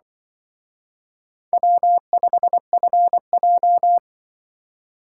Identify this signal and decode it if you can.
Morse code, heard as a keyed tone, about 25 wpm, 700 Hz. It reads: W5FJ
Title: Speed Racing - USA 1x2 Call Signs 20wpm